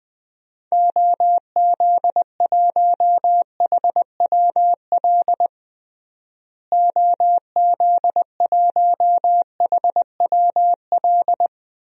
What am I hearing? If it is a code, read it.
Morse code, heard as a keyed tone, about 20 wpm, 700 Hz. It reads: OZ1HWL OZ1HWL
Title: DXCC-100 Call Signs - 6 Characters - Repeated 3x - 20wpm